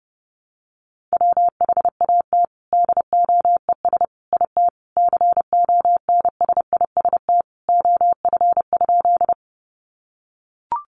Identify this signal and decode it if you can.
Morse code, heard as a keyed tone, about 30 wpm, 700 Hz. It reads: WHAT DOES IT CONSIST OF?
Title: Sentences from Top 1400 Words 30wpm